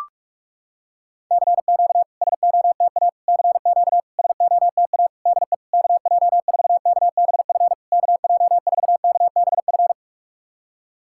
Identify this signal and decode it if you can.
Morse code, heard as a keyed tone, about 44 wpm, 700 Hz. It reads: CQ SOTA CQ SOTA DE KJ4KBF KJ4KBF
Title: SOTA QSO Practice 45wpm